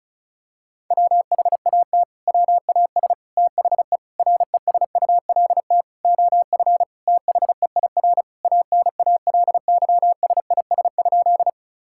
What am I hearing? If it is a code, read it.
Morse code, heard as a keyed tone, about 36 wpm, 700 Hz. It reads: WHAT WAS THE RESULT OF THEIR ANALYSIS?
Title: Sentences from Top 1000 Words - Review Part 5 - 35wpm